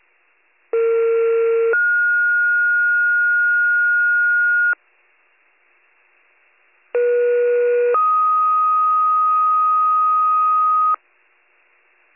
Two-tone page, heard as a one-second tone, then a three-second tone, about 460 then 1430 Hz, about 480 then 1250 Hz. It is heard from 0.7 to 11.0 s.